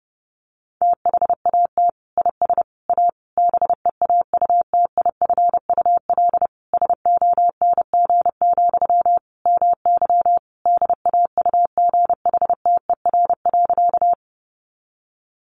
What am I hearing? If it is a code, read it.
Morse code, heard as a keyed tone, about 30 wpm, 700 Hz. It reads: THAT IS A BEAUTIFUL SONG, MY DAUGHTER.
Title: Sentences from Top 1500 Words 30wpm